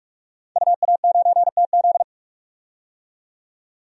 Morse code, heard as a keyed tone, about 44 wpm, 700 Hz. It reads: UA9TZ
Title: DXCC-100 Call Signs - 5 Characters - 45wpm